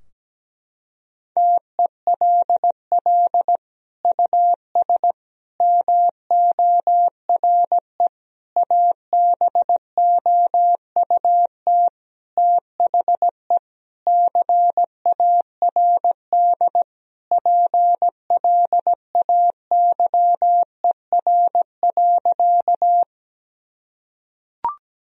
Morse code, heard as a keyed tone, about 17 wpm, 700 Hz. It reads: TELL US MORE ABOUT THE CARD PLAYER.
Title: Sentences from Top 1000 Words - Review Part 4 - 17wpm